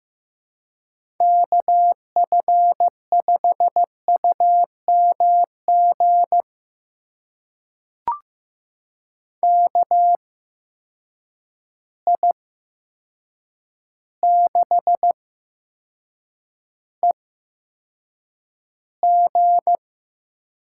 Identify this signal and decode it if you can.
Morse code, heard as a keyed tone, about 15 wpm, spaced as for 7 wpm, 700 Hz. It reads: KF5UMG KI6EG
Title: USA Call Signs - 8x Character Spacing - 15wpm